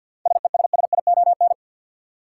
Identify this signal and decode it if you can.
Morse code, heard as a keyed tone, about 53 wpm, 700 Hz. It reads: SESSION
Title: Seven Letter Words - 50wpm